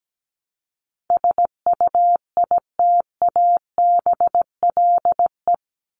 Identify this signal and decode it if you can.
Morse code, heard as a keyed tone, about 17 wpm, 700 Hz. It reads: SUITABLE
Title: Top 1251-1500 Words 17wpm